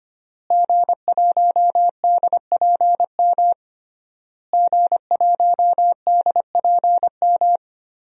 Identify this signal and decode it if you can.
Morse code, heard as a keyed tone, about 25 wpm, 700 Hz. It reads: G1DPM G1DPM
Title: DXCC-100 Call Signs - 5 Characters - Repeated 3x - 25wpm